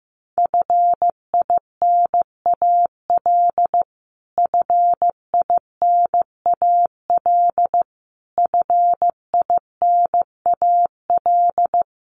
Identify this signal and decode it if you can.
Morse code, heard as a keyed tone, about 15 wpm, 700 Hz. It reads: FINAL FINAL FINAL
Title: Top 500 Words Repeated 15wpm